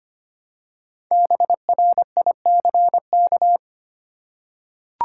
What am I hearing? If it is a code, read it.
Morse code, heard as a keyed tone, about 25 wpm, 700 Hz. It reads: BRICK
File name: practice / 25wpm / Easy Words with Hints - 25wpm